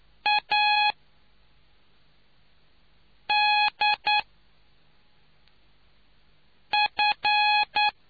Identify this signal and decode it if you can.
Morse code, heard as a keyed tone, about 9 wpm, 840 Hz. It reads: A D F